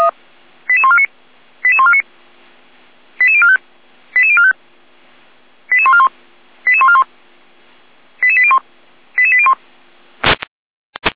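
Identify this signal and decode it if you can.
Five-tone selective call (ZVEI), heard as quick runs of five tones, tones of about 70 ms. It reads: 88149 88149 88045 88045 88141 88141 88801 88801
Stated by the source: ZVEI five-tone selective call received on an SDR.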